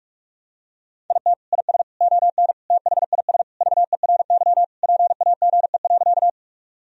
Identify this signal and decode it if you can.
Morse code, heard as a keyed tone, about 46 wpm, 700 Hz. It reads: IT IS ON THIS VERY PAGE.
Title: Sentences from Top 1000 Words - Review Part 1 - 45wpm